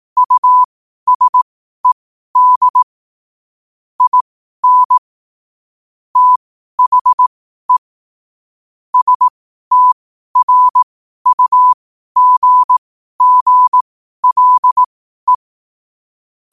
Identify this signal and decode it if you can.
Morse code, heard as a keyed tone, about 18 wpm, spaced as for 12 wpm, 1000 Hz. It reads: USED IN THE STRUGGLE
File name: qotdfiles / 12wpm